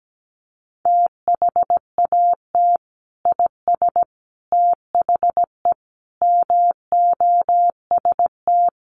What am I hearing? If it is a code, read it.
Morse code, heard as a keyed tone, about 17 wpm, 700 Hz. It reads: THAT IS THE MOST
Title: Sentences from Top 1400 Words 17wpm